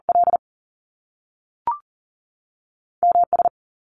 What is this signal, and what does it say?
Morse code, heard as a keyed tone, about 41 wpm, 700 Hz. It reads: L M S
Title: Top Two Letter Combinations in Words 40wpm